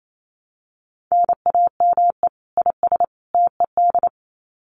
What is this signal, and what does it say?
Morse code, heard as a keyed tone, about 28 wpm, 700 Hz. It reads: NAME IS TED
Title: Common QSO Phrases 28wpm